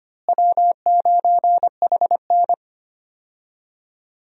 Morse code, heard as a keyed tone, about 25 wpm, 700 Hz. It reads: W9HN
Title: USA Call Signs - 4 Characters - Repeated 3x - 25wpm